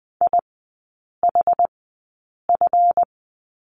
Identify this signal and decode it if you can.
Morse code, heard as a keyed tone, about 20 wpm, spaced as for 8 wpm, 690 Hz. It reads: IHF
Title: 212 Lesson 34e - Number 8 - Three Characters 20wpm